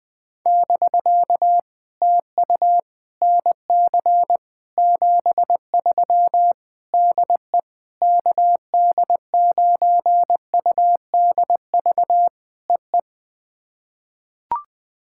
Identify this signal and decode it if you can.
Morse code, heard as a keyed tone, about 21 wpm, 700 Hz. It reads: <BK> TU NC 73 DE KD9UDV EE
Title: POTA QSO Practice - 20wpm